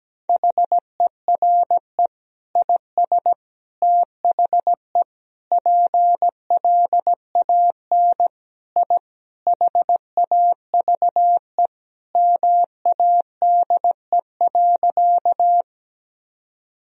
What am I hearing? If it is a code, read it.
Morse code, heard as a keyed tone, about 17 wpm, 700 Hz. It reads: HERE IS THE PLAN I HAVE MADE.
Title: Sentences from Top 1000 Words - Review Part 2 - 17wpm